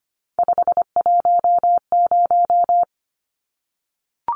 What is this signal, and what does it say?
Morse code, heard as a keyed tone, about 25 wpm, 700 Hz. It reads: 510
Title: Numbers - 3-Digit 25wpm